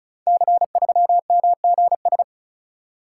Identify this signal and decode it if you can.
Morse code, heard as a keyed tone, about 36 wpm, 700 Hz. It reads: C3MGS
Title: Code Groups - 5 Letter:Numbers - 35wpm